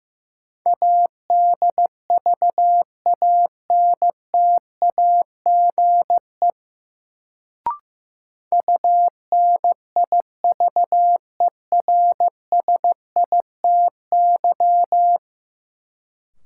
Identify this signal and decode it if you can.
Morse code, heard as a keyed tone, about 16 wpm, 700 Hz. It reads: ADVANTAGE UNIVERSITY
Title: Sets of 4 Words - New Word Drill - 15wpm